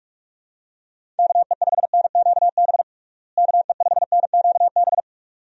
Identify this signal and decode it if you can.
Morse code, heard as a keyed tone, about 45 wpm, 700 Hz. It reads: KE5NQB KE5NQB
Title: USA Call Signs - 6 Characters - Repeated 2x - 45wpm